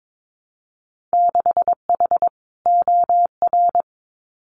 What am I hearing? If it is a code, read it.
Morse code, heard as a keyed tone, about 22 wpm, 700 Hz. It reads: 6H OR
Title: Winter Field Day Exchanges Encore 22wpm